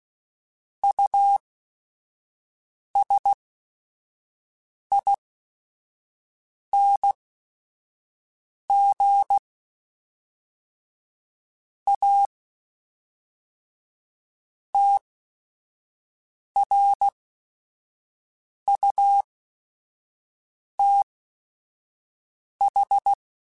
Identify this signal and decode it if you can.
Morse code, heard as a keyed tone, about 16 wpm, spaced as for 5 wpm, 800 Hz. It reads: USING A TRUTH
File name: Technical Subjects